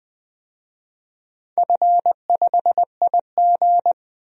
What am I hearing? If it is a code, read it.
Morse code, heard as a keyed tone, about 20 wpm, 700 Hz. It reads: F5IG
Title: DXCC-100 Call Signs - 4 Characters - Repeated 3x - 20wpm